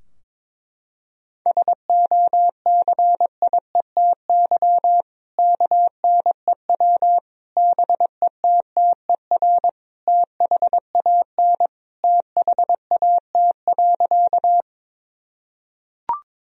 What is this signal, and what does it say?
Morse code, heard as a keyed tone, about 22 wpm, 700 Hz. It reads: SOCIETY KNEW BETTER THAN THAT.
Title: Sentences from Top 1000 Words - Review Part 4 - 22wpm